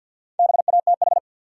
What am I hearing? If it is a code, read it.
Morse code, heard as a keyed tone, about 53 wpm, 700 Hz. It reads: BATH
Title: Sentences from Top 1600 Words - New Word Drill Alt - 50wpm